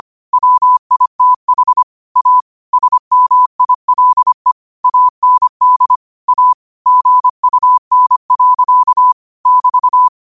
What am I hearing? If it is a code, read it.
Morse code, heard as a keyed tone, about 25 wpm, 1000 Hz. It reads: WITH A SMILE AND A GUN. <BT>